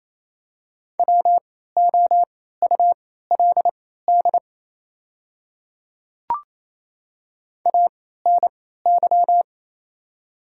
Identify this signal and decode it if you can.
Morse code, heard as a keyed tone, about 27 wpm, spaced as for 16 wpm, 700 Hz. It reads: WOULD ANY
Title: 3x Character Spacing - Top 200 Words - 28wpm